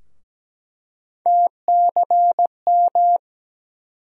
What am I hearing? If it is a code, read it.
Morse code, heard as a keyed tone, about 17 wpm, 700 Hz. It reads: TCM
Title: USA Call Sign Postfixes - 3 Letters 17wpm